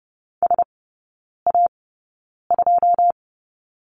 Morse code, heard as a keyed tone, about 30 wpm, spaced as for 9 wpm, 700 Hz. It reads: SA2
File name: Lesson 18e - Letter m - Three Characters 30wpm